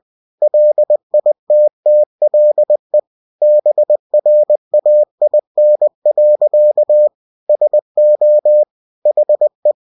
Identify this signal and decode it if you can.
Morse code, heard as a keyed tone, about 20 wpm, 600 Hz. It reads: LITTLE BRAIN. SO HE
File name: cw-books / winnie-the-pooh / winnie-20wpm-page-0045